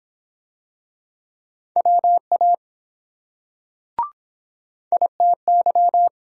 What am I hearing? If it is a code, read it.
Morse code, heard as a keyed tone, about 23 wpm, 700 Hz. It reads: WA ITY